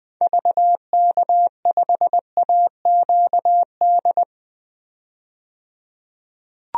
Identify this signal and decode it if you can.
Morse code, heard as a keyed tone, about 20 wpm, 700 Hz. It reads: VK5AQD
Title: DXCC-100 Call Signs - 6 Characters - 20wpm